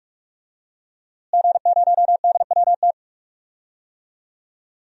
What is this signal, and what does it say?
Morse code, heard as a keyed tone, about 42 wpm, 700 Hz. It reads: G0DWT
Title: DXCC-100 Call Signs - 5 Characters - Repeated 2x - 45wpm